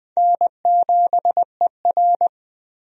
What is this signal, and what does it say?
Morse code, heard as a keyed tone, about 20 wpm, 700 Hz. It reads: N7ER
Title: USA Call Signs - 4 Characters - 20wpm